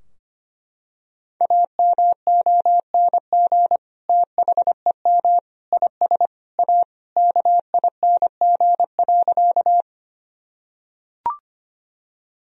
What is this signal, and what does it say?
Morse code, heard as a keyed tone, about 25 wpm, 700 Hz. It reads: AMONG THEM IS A KING.
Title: Sentences from Top 1000 Words - Review Part 3 - 25wpm